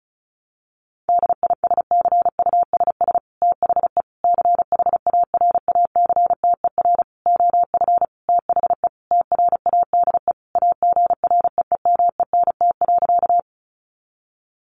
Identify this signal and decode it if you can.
Morse code, heard as a keyed tone, about 34 wpm, 700 Hz. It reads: DISCUSS THE CHARACTER OF THE TRADE AGREEMENT.
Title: Sentences from Top 1500 Words 35wpm